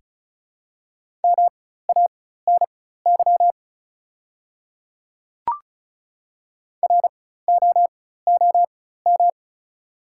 Morse code, heard as a keyed tone, about 36 wpm, spaced as for 16 wpm, 690 Hz. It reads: MANY ROOM
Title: 4x Character Spacing - Top 500 Words - 35wpm